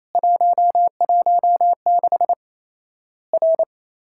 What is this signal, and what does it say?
Morse code, heard as a keyed tone, about 28 wpm, 700 Hz. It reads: 116 R
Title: Worked All Europe DX Contest - QTC Exchange - 28wpm